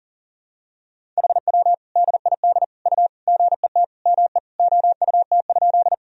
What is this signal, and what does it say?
Morse code, heard as a keyed tone, about 40 wpm, 700 Hz. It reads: HW DID U GET ME OUT?
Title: Sentences from Top 200 Words and Abbreviations 40wpm